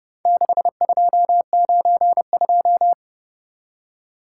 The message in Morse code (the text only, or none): 6292